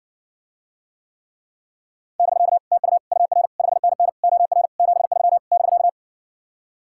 Morse code, heard as a keyed tone, about 60 wpm, 700 Hz. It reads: <BK> TU UR 5NN OR 73 <BK>